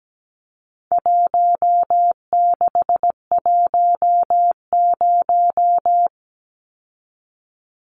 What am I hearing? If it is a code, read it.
Morse code, heard as a keyed tone, about 17 wpm, 700 Hz. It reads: 1610
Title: Numbers - 4-Digit 17wpm